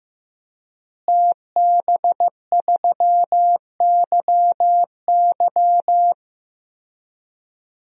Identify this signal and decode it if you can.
Morse code, heard as a keyed tone, about 16 wpm, 700 Hz. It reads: TB3YY
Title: DXCC-100 Call Signs - 5 Characters - Repeated 2x - 15wpm